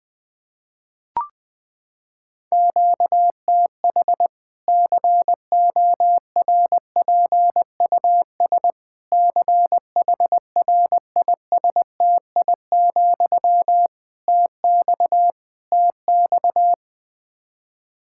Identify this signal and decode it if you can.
Morse code, heard as a keyed tone, about 21 wpm, 700 Hz. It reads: QTH CORPUS CHRISTI, TX TX
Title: Practice QSO - Exchange 1 - Beginning - 20wpm